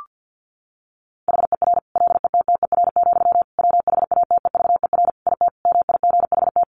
Morse code, heard as a keyed tone, about 49 wpm, 700 Hz. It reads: HER LETTER, WHATEVER IT MIGHT
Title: Sentences from Top 1100 Words 50wpm